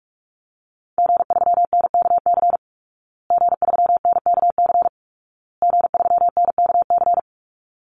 Morse code, heard as a keyed tone, about 44 wpm, 700 Hz. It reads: G3NKC G3NKC G3NKC
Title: Call Signs Repeated 2x - 45wpm